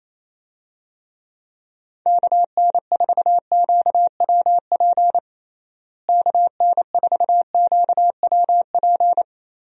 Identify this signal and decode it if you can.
Morse code, heard as a keyed tone, about 28 wpm, 700 Hz. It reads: KN4QWP KN4QWP
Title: USA Call Signs - 6 Characters - Repeated 3x - 28wpm